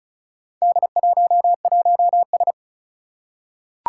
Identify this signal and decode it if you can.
Morse code, heard as a keyed tone, about 36 wpm, 690 Hz. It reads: D11S
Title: Four Challenging Letter-Number Combinations 35wpm